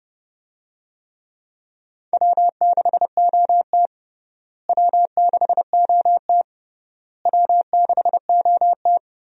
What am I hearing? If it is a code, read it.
Morse code, heard as a keyed tone, about 29 wpm, 700 Hz. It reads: W6OT W6OT W6OT
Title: USA Call Signs - 4 Characters - Repeated 3x - 30wpm